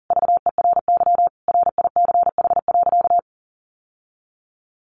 Morse code, heard as a keyed tone, about 40 wpm, 700 Hz. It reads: VERY RICH.
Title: Sentences from Top 1200 Words 40wpm